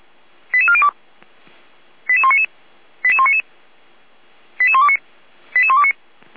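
Five-tone selective call (ZVEI), heard as quick runs of five tones, tones of about 80 ms. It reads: 88402 88199 88199 88129 88129
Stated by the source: ZVEI five-tone selective call received on SDR.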